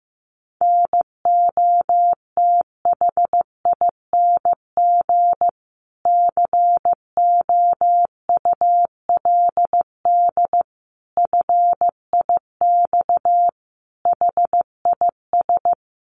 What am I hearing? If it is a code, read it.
Morse code, heard as a keyed tone, about 15 wpm, 700 Hz. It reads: NOTHING COULD FIX HIS
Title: Sentences from Top 1400 Words 15wpm